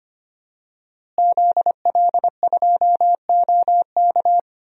Morse code, heard as a keyed tone, about 25 wpm, 700 Hz. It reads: ZL2OK
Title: DXCC-100 Call Signs - 5 Characters - Repeated 3x - 25wpm